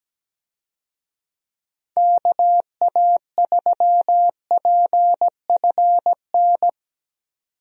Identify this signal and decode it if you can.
Morse code, heard as a keyed tone, about 17 wpm, 700 Hz. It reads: KA3PFN
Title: USA Call Signs - 6 Characters - 17wpm